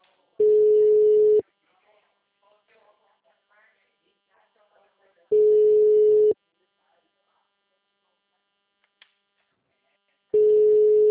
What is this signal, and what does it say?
Ringback tone (Europe, 425 Hz), with speech.
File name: out- -2033- - - .35246